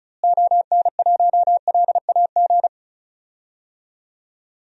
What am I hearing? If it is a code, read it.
Morse code, heard as a keyed tone, about 34 wpm, 700 Hz. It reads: ON1LAG
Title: DXCC-100 Call Signs - 6 Characters - Repeated 3x - 35wpm